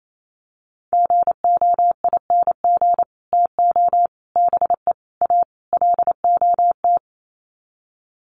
Morse code, heard as a keyed tone, about 28 wpm, 700 Hz. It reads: GOING TO BE A LOT
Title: Sets of 5 Words Encore 28wpm